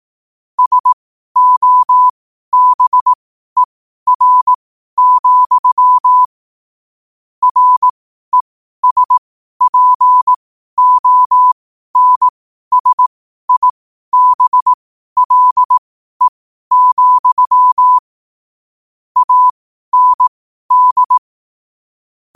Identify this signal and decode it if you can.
Morse code, heard as a keyed tone, about 18 wpm, spaced as for 12 wpm, 1000 Hz. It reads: SOBER, RESPONSIBLE, AND
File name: qotdfiles / 12wpm